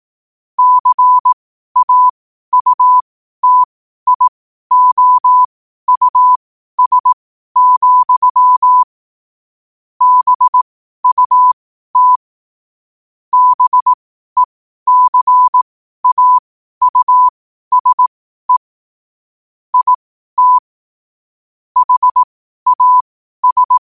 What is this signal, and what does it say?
Morse code, heard as a keyed tone, about 18 wpm, spaced as for 12 wpm, 1000 Hz. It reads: CAUTIOUS, BUT BECAUSE IT HAS
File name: qotdfiles / 12wpm